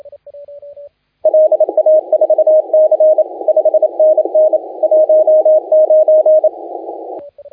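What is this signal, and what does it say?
Morse code, heard as a keyed tone, about 29 wpm, 620 Hz, with radio noise. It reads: LA4C 5NN 19
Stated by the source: Morse contest audio recorded by LA4C during the SAC CW 2018 contest.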